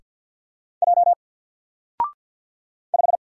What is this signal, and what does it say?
Morse code, heard as a keyed tone, about 50 wpm, 700 Hz. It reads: J 5
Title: Single Letter-Number 50wpm